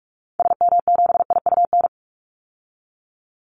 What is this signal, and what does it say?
Morse code, heard as a keyed tone, about 44 wpm, 700 Hz. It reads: SM7IUN
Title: Call Signs Repeated 3x - 45wpm